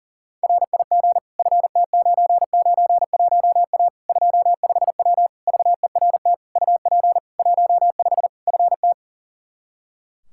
Morse code, heard as a keyed tone, about 40 wpm, 700 Hz. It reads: RIG FT991A 25W VERT UP 15 FT
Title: Practice QSO - Exchange 2 - Info - 40wpm